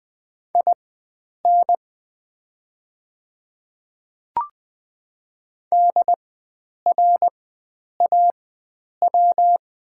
Morse code, heard as a keyed tone, about 20 wpm, spaced as for 9 wpm, 700 Hz. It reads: IN DRAW
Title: 4x Character Spacing - Top 300 Words - 20wpm